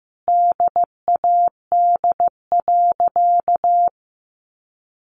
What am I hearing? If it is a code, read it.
Morse code, heard as a keyed tone, about 15 wpm, 700 Hz. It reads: DAD.
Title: Easy Phrases 15wpm